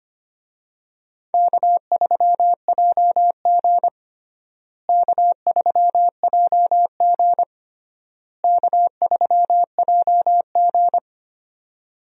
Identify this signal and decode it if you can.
Morse code, heard as a keyed tone, about 25 wpm, 700 Hz. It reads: K3JG K3JG K3JG
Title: USA Call Signs - 4 Characters - Repeated 3x - 25wpm